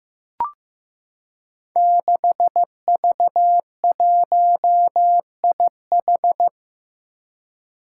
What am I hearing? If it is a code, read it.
Morse code, heard as a keyed tone, about 15 wpm, 700 Hz. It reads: 6V1IH
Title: Five Challenging Letter-Number Combinations 15wpm